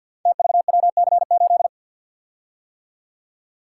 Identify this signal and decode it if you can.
Morse code, heard as a keyed tone, about 53 wpm, 700 Hz. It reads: TVWC8
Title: Code Groups - 5 Letter:Numbers - 50wpm